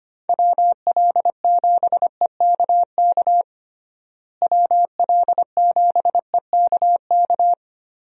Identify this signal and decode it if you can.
Morse code, heard as a keyed tone, about 25 wpm, 700 Hz. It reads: WL7EKK WL7EKK
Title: USA Call Signs - 6 Characters - Repeated 3x - 25wpm